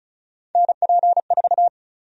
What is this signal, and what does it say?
Morse code, heard as a keyed tone, about 34 wpm, 700 Hz. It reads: NP4
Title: USA Call Sign Prefixes - 35wpm